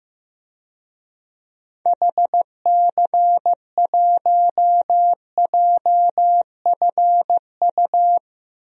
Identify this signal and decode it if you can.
Morse code, heard as a keyed tone, about 15 wpm, 700 Hz. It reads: HC1JFU
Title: DXCC-100 Call Signs - 6 Characters - Repeated 3x - 15wpm